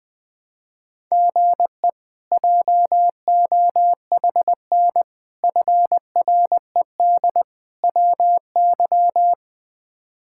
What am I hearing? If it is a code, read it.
Morse code, heard as a keyed tone, about 20 wpm, 700 Hz. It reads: GE JOHN FRED WY
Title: K1USN SST QSO Practice - 20wpm